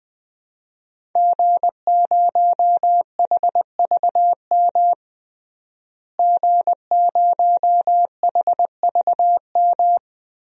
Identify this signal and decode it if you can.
Morse code, heard as a keyed tone, about 21 wpm, 700 Hz. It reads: G0HVM G0HVM
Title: DXCC-100 Call Signs - 5 Characters - Repeated 3x - 20wpm